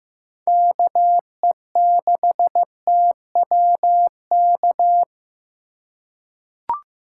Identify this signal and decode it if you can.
Morse code, heard as a keyed tone, about 15 wpm, 700 Hz. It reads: KE6TWK
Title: USA Call Signs - 6 Characters - 4x Character Spacing - 15wpm